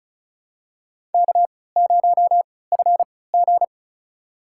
Morse code, heard as a keyed tone, about 36 wpm, spaced as for 20 wpm, 700 Hz. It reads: K0FG
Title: USA Call Signs - 4 Characters - 3x Character Spacing - 35wpm